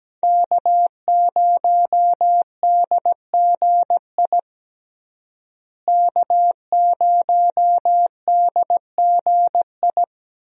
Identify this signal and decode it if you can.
Morse code, heard as a keyed tone, about 17 wpm, 700 Hz. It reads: K0DGI K0DGI
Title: USA Call Signs - 5 Characters - Repeated 2x - 17wpm